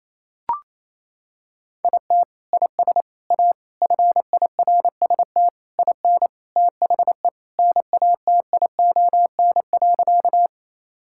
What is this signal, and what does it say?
Morse code, heard as a keyed tone, about 27 wpm, 700 Hz. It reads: IT IS A FIRST IN THE NATION.
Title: Sentences from Top 1000 Words - Review Part 6 - 28wpm